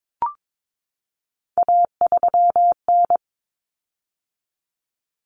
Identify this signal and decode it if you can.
Morse code, heard as a keyed tone, about 22 wpm, 700 Hz. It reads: A3N
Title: Common Cut Numbers - 3-Digit 22wpm